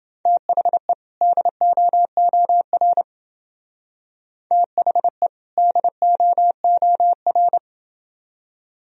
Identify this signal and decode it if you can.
Morse code, heard as a keyed tone, about 29 wpm, 700 Hz. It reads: THE DOOR THE DOOR